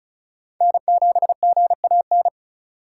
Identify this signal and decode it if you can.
Morse code, heard as a keyed tone, about 36 wpm, 700 Hz. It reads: N7GAN
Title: USA Call Signs - 5 Characters - 2x Character Spacing - 35wpm